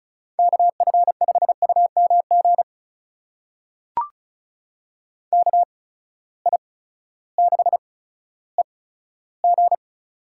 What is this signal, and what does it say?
Morse code, heard as a keyed tone, about 36 wpm, spaced as for 16 wpm, 700 Hz. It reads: KF5UMG KI6EG
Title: USA Call Signs - 8x Character Spacing - 35wpm